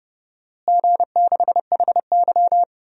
Morse code, heard as a keyed tone, about 31 wpm, 700 Hz. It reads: G6HY
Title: DXCC-100 Call Signs - 4 Characters - Repeated 2x - 30wpm